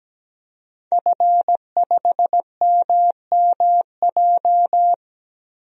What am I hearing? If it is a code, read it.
Morse code, heard as a keyed tone, about 17 wpm, 700 Hz. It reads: F5MMJ
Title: DXCC-100 Call Signs - 5 Characters - 17wpm